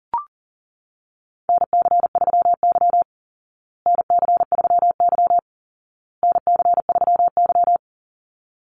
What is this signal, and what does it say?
Morse code, heard as a keyed tone, about 40 wpm, 700 Hz. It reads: NC3Y NC3Y NC3Y
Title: Call Signs Repeated 2x - 40wpm